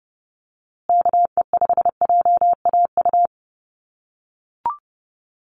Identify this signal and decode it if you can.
Morse code, heard as a keyed tone, about 30 wpm, 700 Hz. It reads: KE5JAU
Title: USA Call Signs Repeated 2x 30wpm